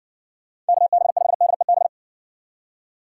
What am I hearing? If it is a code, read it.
Morse code, heard as a keyed tone, about 58 wpm, 700 Hz. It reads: KB5DEB